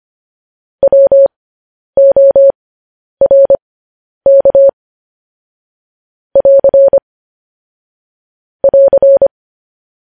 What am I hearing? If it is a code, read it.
Morse code, heard as a keyed tone, about 25 wpm, spaced as for 10 wpm, 550 Hz. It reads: WORK <AR> <AR>
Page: news headlines in morse code at 10 wpm